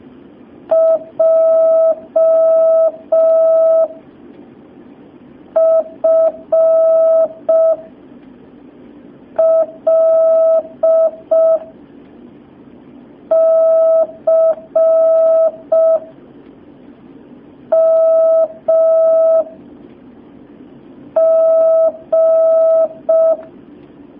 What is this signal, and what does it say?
Morse code, heard as a keyed tone, about 5 wpm, 650 Hz. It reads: J F L C M G
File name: jh8cbh / 2019cw